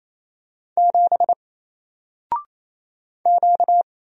Morse code, heard as a keyed tone, about 27 wpm, 700 Hz. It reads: 7 Q
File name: Single-Letter-Number-28wpm